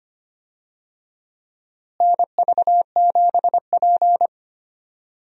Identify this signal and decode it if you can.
Morse code, heard as a keyed tone, about 25 wpm, 700 Hz. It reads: NV7P
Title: USA Call Signs - 4 Characters - Repeated 3x - 25wpm